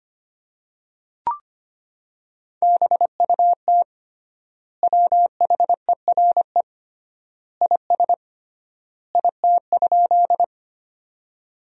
Morse code, heard as a keyed tone, about 25 wpm, 700 Hz. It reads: BUT WHERE IS IT?
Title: 3x Word-Spacing - Sentences from Top 200 Words - 25wpm